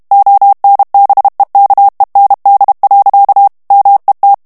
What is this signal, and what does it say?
Morse code, heard as a keyed tone, about 32 wpm, 800 Hz. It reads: ONBEKEND. MET
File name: HOLLANDAIS texte clair 1 vitesse 30 mots minute